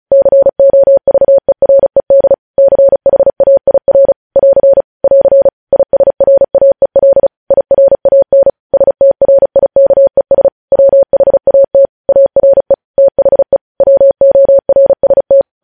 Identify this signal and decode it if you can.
Morse code, heard as a keyed tone, about 36 wpm, 550 Hz. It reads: COVERED CHAIR <AR> <AR> ISRAEL IRAN STRIKES WHAT ARE THE WORST
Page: news headlines in morse code at 35 wpm